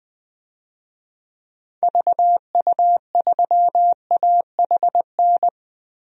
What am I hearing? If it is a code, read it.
Morse code, heard as a keyed tone, about 20 wpm, 700 Hz. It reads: VU3AHN
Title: DXCC-100 Call Signs - 6 Characters - 20wpm